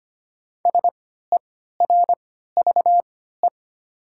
Morse code, heard as a keyed tone, about 25 wpm, spaced as for 14 wpm, 690 Hz. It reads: SERVE
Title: 3x Character Spacing - Top 500 Words - 25wpm